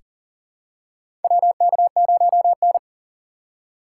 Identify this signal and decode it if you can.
Morse code, heard as a keyed tone, about 39 wpm, 700 Hz. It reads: WK0N
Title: USA Call Signs - 2x Character Spacing - 40wpm